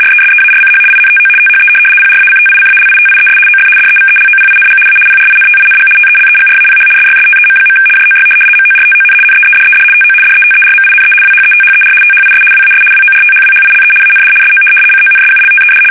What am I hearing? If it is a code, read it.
RTTY, about 75 baud. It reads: RESEARCH INTO THE PRACTICALITIES OF DEVELOPING A PRINTING TELEGRAPH SYSTEM. JOY MORTON NEEDED TO DETERMINE WHETHER THIS WAS WORTHWHILE AND SO CONSULTED
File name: fsk850-75B / teletype-corp